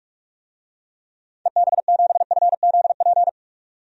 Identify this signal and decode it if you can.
Morse code, heard as a keyed tone, about 45 wpm, 700 Hz. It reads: EB7FZP
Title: DXCC-100 Call Signs - 6 Characters - 45wpm